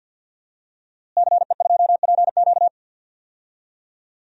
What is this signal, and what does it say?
Morse code, heard as a keyed tone, about 48 wpm, 700 Hz. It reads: CE2PQ
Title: DXCC-100 Call Signs - 5 Characters - Repeated 2x - 50wpm